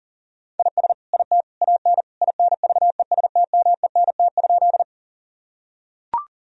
Morse code, heard as a keyed tone, about 40 wpm, 700 Hz. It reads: IS IT AN INVESTMENT?
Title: Sentences from Top 1000 Words - Review Part 4 - 40wpm